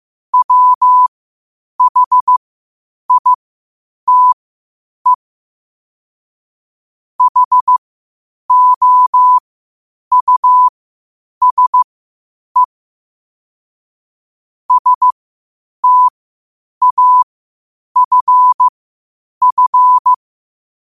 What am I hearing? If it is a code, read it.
Morse code, heard as a keyed tone, about 15 wpm, spaced as for 8 wpm, 1000 Hz. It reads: WHITE HOUSE STAFF
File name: qotdfiles / 8wpm